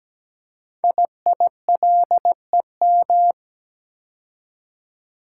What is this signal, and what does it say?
Morse code, heard as a keyed tone, about 17 wpm, 700 Hz. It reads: IILEM
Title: Code Groups - 5 Letter:Numbers - 17wpm